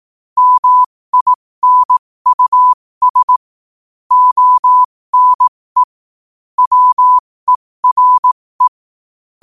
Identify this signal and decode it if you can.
Morse code, heard as a keyed tone, about 18 wpm, spaced as for 15 wpm, 1000 Hz. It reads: MINUS ONE WERE
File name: qotdfiles / 15wpm